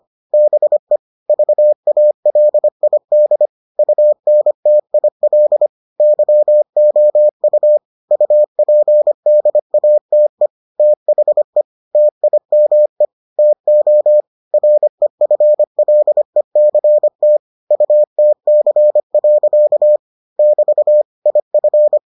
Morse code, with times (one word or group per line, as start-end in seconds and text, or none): 0.3-1.0 BE
1.3-3.5 VALID
3.8-5.7 UNTIL
6.0-7.8 YOU
8.1-10.5 UPDATE
10.8-11.6 THE
11.9-13.0 TIME
13.4-14.2 TO
14.5-17.4 REFLECT
17.7-20.0 UTC.
20.4-21.0 <BT>
21.3-22.0 IF